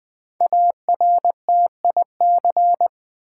Morse code, heard as a keyed tone, about 20 wpm, 700 Hz. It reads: ARTIC
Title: Build A Word - 5 Letters At A Time - 10-Letter Words - 20wpm